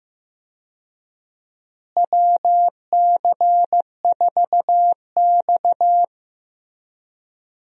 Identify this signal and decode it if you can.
Morse code, heard as a keyed tone, about 16 wpm, 700 Hz. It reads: WC4X
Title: USA Call Signs - 4 Characters - Repeated 2x - 15wpm